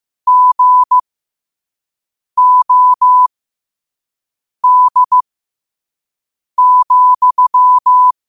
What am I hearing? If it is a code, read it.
Morse code, heard as a keyed tone, about 15 wpm, spaced as for 5 wpm, 1000 Hz. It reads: GOD,
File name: qotdfiles / 5wpm